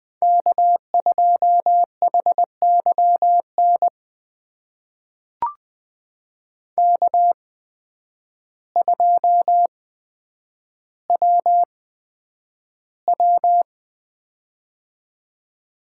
Morse code, heard as a keyed tone, about 20 wpm, spaced as for 9 wpm, 700 Hz. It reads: K2HYN K2WW
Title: USA Call Signs - 8x Character Spacing - 20wpm